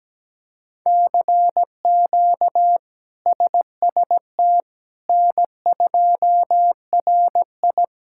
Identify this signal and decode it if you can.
Morse code, heard as a keyed tone, about 17 wpm, 700 Hz. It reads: CQ SST N2RI
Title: K1USN SST QSO Practice - 17wpm